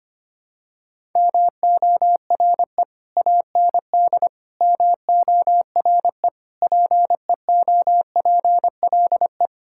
Morse code, heard as a keyed tone, about 25 wpm, 700 Hz. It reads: MORE AND MORE PEOPLE
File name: practice / 25wpm / Sentences with 3-Word Phrases - 25wpm